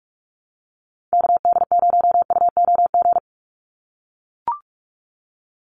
Morse code, heard as a keyed tone, about 44 wpm, 700 Hz. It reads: KD0UOG
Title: USA Call Signs 45wpm